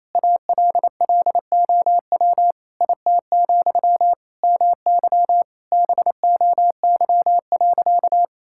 Morse code, heard as a keyed tone, about 28 wpm, 700 Hz. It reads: ALLOW IT, MY BOY.
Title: Sentences from Top 1000 Words - Review Part 5 - 28wpm